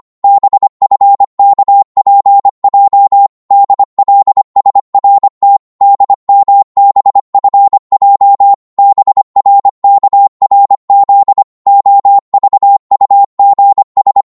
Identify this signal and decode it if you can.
Morse code, heard as a keyed tone, about 25 wpm, 800 Hz. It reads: BFKPJ DLSRT DMBFJ BRKRZ OVUGS
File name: Morse / ltr-25wpm